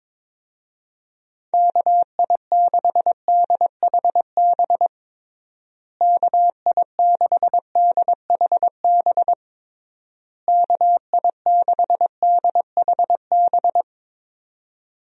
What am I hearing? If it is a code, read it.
Morse code, heard as a keyed tone, about 22 wpm, 700 Hz. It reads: KI6DHB KI6DHB KI6DHB
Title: USA Call Signs - 6 Characters - Repeated 3x - 22wpm